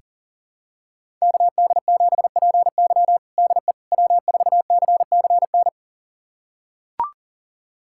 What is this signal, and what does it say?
Morse code, heard as a keyed tone, about 40 wpm, 700 Hz. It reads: KD7PY DE W4CCN
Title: USA Callsign DE Callsign - 5 Characters - 40wpm